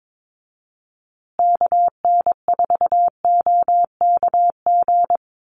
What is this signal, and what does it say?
Morse code, heard as a keyed tone, about 22 wpm, 700 Hz. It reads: KN4OKG
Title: USA Call Signs Repeated 3x 22wpm